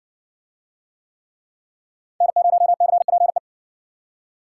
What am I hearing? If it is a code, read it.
Morse code, heard as a keyed tone, about 56 wpm, 700 Hz. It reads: N0CPE